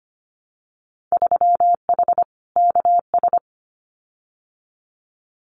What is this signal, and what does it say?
Morse code, heard as a keyed tone, about 25 wpm, 700 Hz. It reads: 3H KS
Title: Winter Field Day Exchanges 25wpm